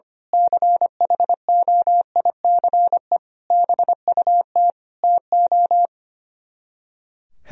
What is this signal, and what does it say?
Morse code, heard as a keyed tone, about 24 wpm, 700 Hz. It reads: CHOICE BUT TO
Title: Speed-Racing - Sets of 5 Words 25wpm